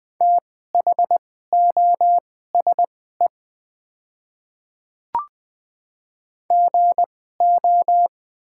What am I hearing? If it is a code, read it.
Morse code, heard as a keyed tone, about 20 wpm, spaced as for 14 wpm, 690 Hz. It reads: THOSE GO
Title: 2x Character Spacing - Top 300 Words - 20wpm